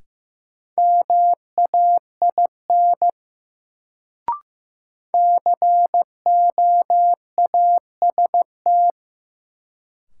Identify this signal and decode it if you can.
Morse code, heard as a keyed tone, about 16 wpm, 700 Hz. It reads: MAIN COAST
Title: Sentences from Top 1300 Words - New Word Drill Alt - 15wpm